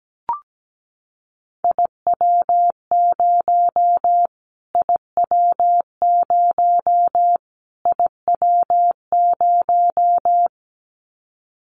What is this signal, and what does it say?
Morse code, heard as a keyed tone, about 17 wpm, 700 Hz. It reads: IW0 IW0 IW0
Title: Callsign Prefixes for Top 25 DXCC - Repeated 3x - 17wpm